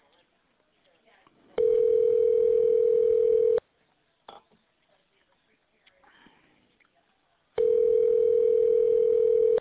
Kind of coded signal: telephone call progress tone